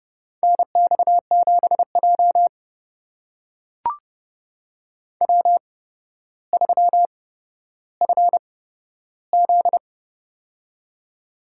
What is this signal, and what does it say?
Morse code, heard as a keyed tone, about 30 wpm, spaced as for 13 wpm, 700 Hz. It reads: NX7J W3FZ
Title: USA Call Signs - 4 Characters - 8x Character Spacing - 30wpm